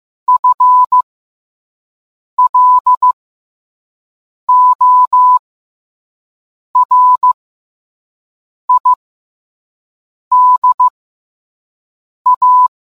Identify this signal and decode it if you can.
Morse code, heard as a keyed tone, about 15 wpm, spaced as for 5 wpm, 1000 Hz. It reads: FLORIDA